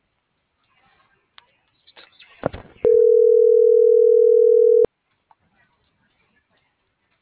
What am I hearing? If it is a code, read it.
Ringback tone (North America), 440 + 480 Hz.
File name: CENTRAL MULTIMODAL / out josimar.cruz - .70324